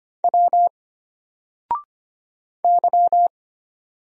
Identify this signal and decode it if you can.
Morse code, heard as a keyed tone, about 25 wpm, 700 Hz. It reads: W Y
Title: Single Letter-Number Alt 25wpm